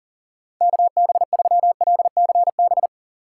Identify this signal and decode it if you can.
Morse code, heard as a keyed tone, about 40 wpm, 700 Hz. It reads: KB3LCB
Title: USA Call Signs - 6 Characters - 2x Character Spacing - 40wpm